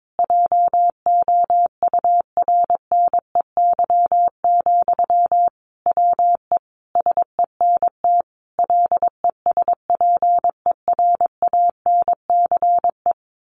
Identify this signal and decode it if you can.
Morse code, heard as a keyed tone, about 22 wpm, 700 Hz. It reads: JOURNEY, WE SENT LESPERANCE
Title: A Sound Of Thunder 22wpm